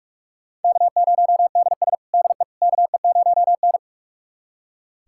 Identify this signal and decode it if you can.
Morse code, heard as a keyed tone, about 41 wpm, 700 Hz. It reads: K0DS DE KE0N
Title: USA Callsign DE Callsign - 4 Characters - 45wpm